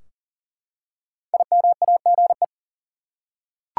Morse code, heard as a keyed tone, about 40 wpm, 700 Hz. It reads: IMAGE
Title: Five Letter Words - 40wpm